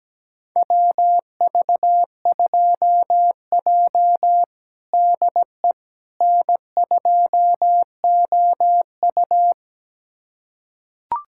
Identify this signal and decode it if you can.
Morse code, heard as a keyed tone, about 17 wpm, 700 Hz. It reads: WV2J DE N2OU
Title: USA Callsign DE Callsign - 4 Characters - 17wpm